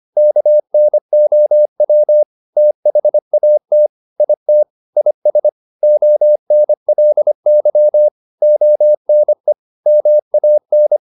Morse code, heard as a keyed tone, about 25 wpm, 600 Hz. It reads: KNOW THAT IT IS ONLY ONE MAN
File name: the old man and the sea part 1 25wpm